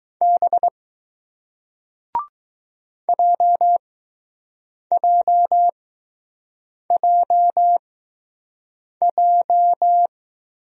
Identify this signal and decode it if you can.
Morse code, heard as a keyed tone, about 20 wpm, spaced as for 6 wpm, 700 Hz. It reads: B JJJJ